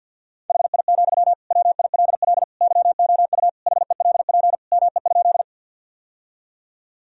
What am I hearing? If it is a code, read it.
Morse code, heard as a keyed tone, about 47 wpm, 700 Hz. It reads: HI, WILL YOU HELP ME?
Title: Sentences from Top 1000 Words - Review Part 3 - 50wpm